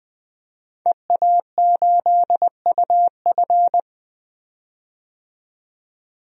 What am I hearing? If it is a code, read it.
Morse code, heard as a keyed tone, about 20 wpm, 700 Hz. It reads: EA8UF
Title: DXCC-100 Call Signs - 5 Characters - 20wpm